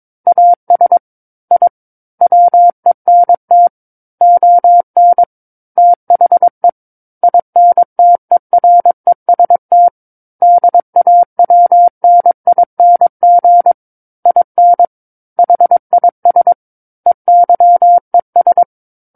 Morse code, heard as a keyed tone, about 22 wpm, 700 Hz. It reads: AS I WENT ON THE INTEREST DAWNING IN HIS EYES
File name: Book One / 13-Chapter 13 - 22 WPM